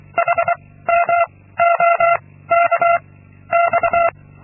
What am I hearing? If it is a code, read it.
Morse code, heard as a keyed tone, about 23 wpm, 680 Hz. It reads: H M O K X
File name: Linksmakalnis-2